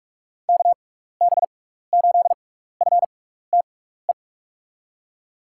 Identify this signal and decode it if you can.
Morse code, heard as a keyed tone, about 44 wpm, spaced as for 15 wpm, 700 Hz. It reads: KB8FTE
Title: USA Call Signs - 6 Characters - 6x Character Spacing - 45wpm